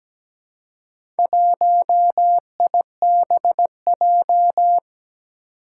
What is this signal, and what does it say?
Morse code, heard as a keyed tone, about 17 wpm, 700 Hz. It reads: 1IBJ
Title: Four Challenging Letter-Number Combinations 17wpm